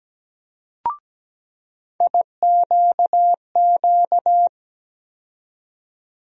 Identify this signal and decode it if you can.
Morse code, heard as a keyed tone, about 17 wpm, 700 Hz. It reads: IQQ
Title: USA Call Sign Postfixes - 3 Letters 17wpm